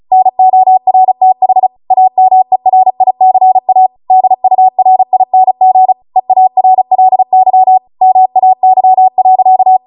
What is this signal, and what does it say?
Morse code, heard as a keyed tone, about 36 wpm, 750 Hz. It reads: NORTH AMERICA DURING EARLY MAY.